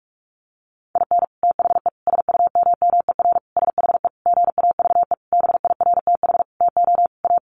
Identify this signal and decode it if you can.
Morse code, heard as a keyed tone, about 44 wpm, 700 Hz. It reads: IN THE SUMMER SHE GAVE BIRTH TO A
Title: Sentences from Top 1100 Words 45wpm